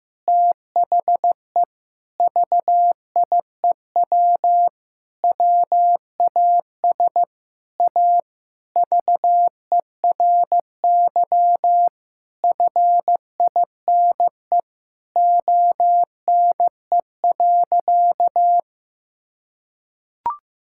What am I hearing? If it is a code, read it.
Morse code, heard as a keyed tone, about 16 wpm, 700 Hz. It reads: THE VIEW WAS A VERY FINE ONE.
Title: Sentences from Top 1000 Words - Review Part 6 - 15wpm